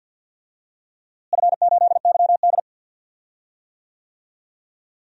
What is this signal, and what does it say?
Morse code, heard as a keyed tone, about 48 wpm, 700 Hz. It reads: F8YD
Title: DXCC-100 Call Signs - 4 Characters - Repeated 2x - 50wpm